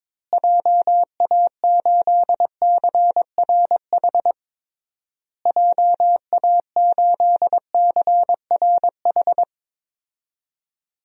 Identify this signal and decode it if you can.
Morse code, heard as a keyed tone, about 22 wpm, 700 Hz. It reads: JA8CRH JA8CRH
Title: DXCC-100 Call Signs - 6 Characters - Repeated 3x - 22wpm